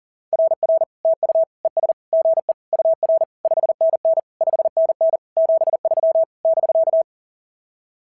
Morse code, heard as a keyed tone, about 37 wpm, 660 Hz. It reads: RR TU ES GE UR 5NN 5NN 73 <BK>
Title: SOTA QSO Practice 40wpm